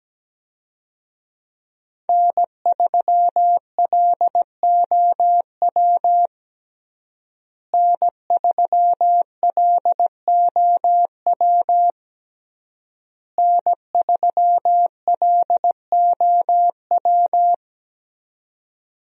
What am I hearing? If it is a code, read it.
Morse code, heard as a keyed tone, about 17 wpm, 700 Hz. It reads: N3LOW N3LOW N3LOW
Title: USA Call Signs - 5 Characters - Repeated 3x - 17wpm